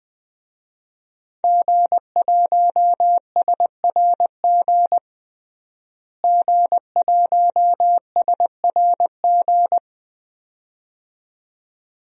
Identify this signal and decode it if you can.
Morse code, heard as a keyed tone, about 20 wpm, 700 Hz. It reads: G1SRG G1SRG
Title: DXCC-100 Call Signs - 5 Characters - Repeated 2x - 20wpm